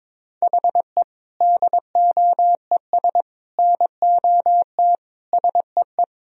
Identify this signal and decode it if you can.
Morse code, heard as a keyed tone, about 22 wpm, 700 Hz. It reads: HE DOES NOT SEE
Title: Sentences from Top 1000 Words - Review Part 1 - 22wpm